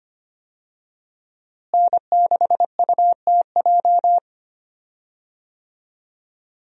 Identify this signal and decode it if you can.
Morse code, heard as a keyed tone, about 25 wpm, 700 Hz. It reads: N6UTJ
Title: USA Call Signs - 5 Characters - Repeated 2x - 25wpm